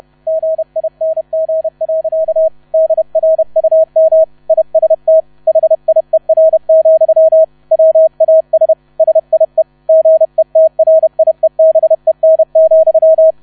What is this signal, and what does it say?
Morse code, heard as a keyed tone, about 31 wpm, 640 Hz. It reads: GING. DRUM IST HIER, WAS SIE GETRIEBEN,
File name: 30wpm